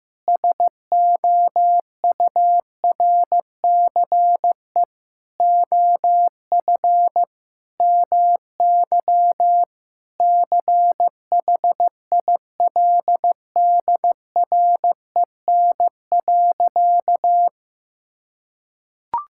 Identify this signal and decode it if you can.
Morse code, heard as a keyed tone, about 15 wpm, 700 Hz. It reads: SOURCE OF MY CHILDREN.
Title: Sentences from Top 1000 Words - Review Part 5 - 15wpm